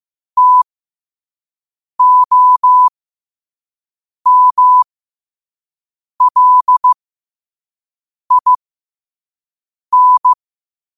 Morse code, heard as a keyed tone, about 15 wpm, spaced as for 5 wpm, 1000 Hz. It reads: TOMLIN
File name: qotdfiles / 5wpm